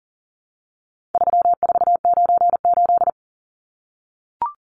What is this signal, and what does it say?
Morse code, heard as a keyed tone, about 40 wpm, 700 Hz. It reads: 3498
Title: Numbers - 4-Digit 40wpm